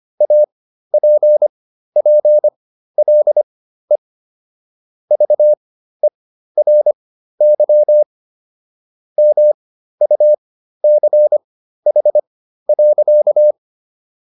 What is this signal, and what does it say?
Morse code, heard as a keyed tone, about 25 wpm, spaced as for 13 wpm, 600 Hz. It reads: APPLE VERY MUCH.